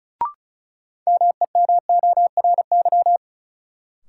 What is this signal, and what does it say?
Morse code, heard as a keyed tone, about 34 wpm, 700 Hz. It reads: MEMORY
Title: Sentences from Top 1300 Words - New Word Drill - 35wpm